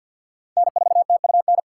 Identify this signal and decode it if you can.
Morse code, heard as a keyed tone, about 47 wpm, 700 Hz. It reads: N4TUN
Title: USA Call Signs - 5 Characters - 2x Character Spacing - 50wpm